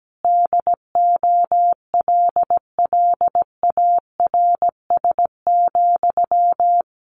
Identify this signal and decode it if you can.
Morse code, heard as a keyed tone, about 17 wpm, 700 Hz. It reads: DOLLARS,
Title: A Sound Of Thunder 17wpm